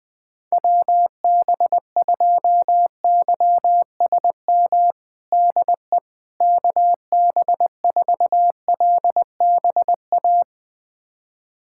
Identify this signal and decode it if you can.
Morse code, heard as a keyed tone, about 20 wpm, 700 Hz. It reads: WB2YSM DE KB4LBA
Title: USA Callsign DE Callsign 20wpm